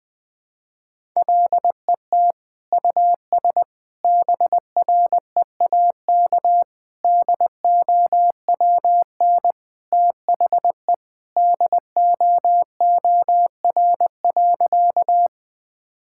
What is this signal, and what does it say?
Morse code, heard as a keyed tone, about 20 wpm, 700 Hz. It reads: LET US BREAK DOWN THE DOOR.
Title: Sentences from Top 1000 Words - Review Part 6 - 20wpm